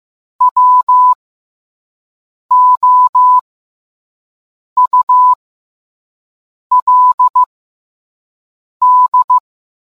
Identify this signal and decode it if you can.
Morse code, heard as a keyed tone, about 15 wpm, spaced as for 5 wpm, 1000 Hz. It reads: WOULD